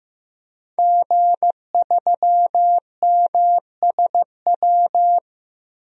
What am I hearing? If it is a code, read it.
Morse code, heard as a keyed tone, about 16 wpm, 700 Hz. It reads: G3MSW
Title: DXCC-100 Call Signs - 5 Characters - Repeated 3x - 15wpm